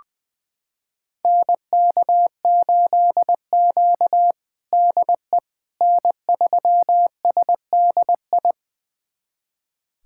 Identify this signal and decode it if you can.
Morse code, heard as a keyed tone, about 20 wpm, 700 Hz. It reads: NK8Q DE N3SDI
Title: USA Callsign DE Callsign 20wpm